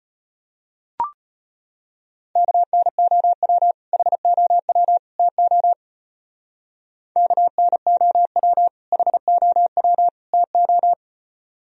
Morse code, heard as a keyed tone, about 32 wpm, 700 Hz. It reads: KNOW HOW TO KNOW HOW TO